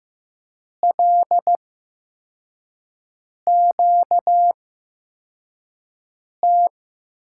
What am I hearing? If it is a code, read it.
Morse code, heard as a keyed tone, about 15 wpm, spaced as for 4 wpm, 700 Hz. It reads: LQT
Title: USA Call Signs - 5 Characters - 8x Character Spacing - 15wpm